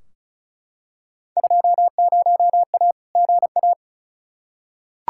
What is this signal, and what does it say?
Morse code, heard as a keyed tone, about 34 wpm, 700 Hz. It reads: 20A GA
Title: Field Day Exchanges Encore - 35wpm